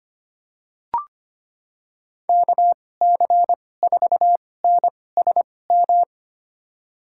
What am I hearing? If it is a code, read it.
Morse code, heard as a keyed tone, about 25 wpm, spaced as for 18 wpm, 700 Hz. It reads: KC4NSM
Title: USA Call Signs - 6 Characters - 2x Character Spacing - 25wpm